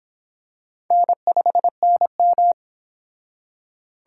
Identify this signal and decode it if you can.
Morse code, heard as a keyed tone, about 24 wpm, 700 Hz. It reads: N5NM